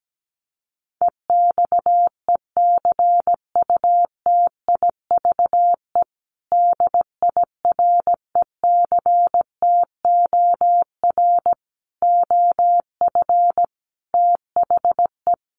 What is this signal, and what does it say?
Morse code, heard as a keyed tone, about 17 wpm, 700 Hz. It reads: EXECUTIVE DIRECTOR OF THE
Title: Sets of 5 Words Encore2 17wpm